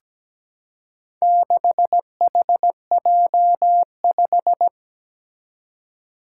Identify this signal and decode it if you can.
Morse code, heard as a keyed tone, about 17 wpm, 700 Hz. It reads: 6HJ5
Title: Four Challenging Letter-Number Combinations 17wpm